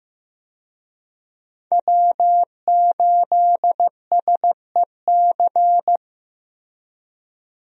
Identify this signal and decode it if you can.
Morse code, heard as a keyed tone, about 14 wpm, 700 Hz. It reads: W8SEC
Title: USA Call Signs - 5 Characters - Repeated 2x - 15wpm